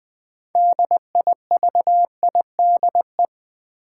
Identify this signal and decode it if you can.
Morse code, heard as a keyed tone, about 20 wpm, 700 Hz. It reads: DIVIDE
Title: Sentences from Top 1600 Words - New Word Drill Alt - 20wpm